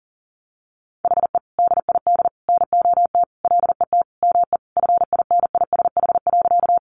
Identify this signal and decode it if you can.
Morse code, heard as a keyed tone, about 40 wpm, 700 Hz. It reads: HE DID NOT LET ME FINISH.
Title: Sentences from Top 1200 Words 40wpm